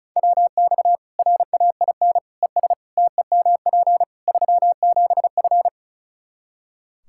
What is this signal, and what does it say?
Morse code, heard as a keyed tone, about 36 wpm, 700 Hz. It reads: WX RAIN ES TEMP 37F
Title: Practice QSO - Exchange 2 - Info - 35wpm